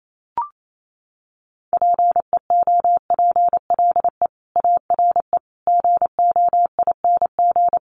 Morse code, heard as a keyed tone, about 28 wpm, 700 Hz. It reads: PEOPLE ARE GOING
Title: Sets of 4 Words 28wpm